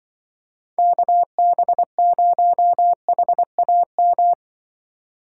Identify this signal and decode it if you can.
Morse code, heard as a keyed tone, about 25 wpm, 700 Hz. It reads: KB0HAM
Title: Speed Racing - USA 2x3 Call Signs 20wpm